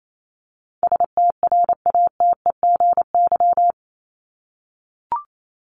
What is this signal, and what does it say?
Morse code, heard as a keyed tone, about 28 wpm, 700 Hz. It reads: STRATEGY
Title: Top 501-750 Words 28wpm